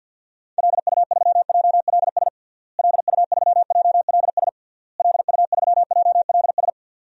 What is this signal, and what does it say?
Morse code, heard as a keyed tone, about 47 wpm, 700 Hz. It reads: LU3JLS LU3JLS LU3JLS
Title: DXCC-100 Call Signs - 6 Characters - Repeated 3x - 50wpm